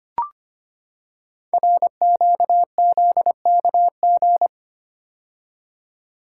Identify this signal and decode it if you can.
Morse code, heard as a keyed tone, about 25 wpm, 700 Hz. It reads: RQZKG
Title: Code Groups - 5 Letter:Numbers - 25wpm